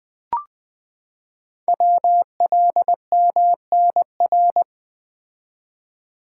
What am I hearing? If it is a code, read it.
Morse code, heard as a keyed tone, about 20 wpm, 700 Hz. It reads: WLMNR
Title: Code Groups - 5 Letters - 20wpm